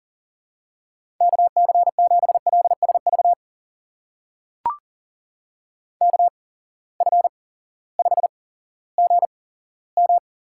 Morse code, heard as a keyed tone, about 40 wpm, spaced as for 18 wpm, 700 Hz. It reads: KC7LSV KF5GM
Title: USA Call Signs - 6 Characters - 8x Character Spacing - 40wpm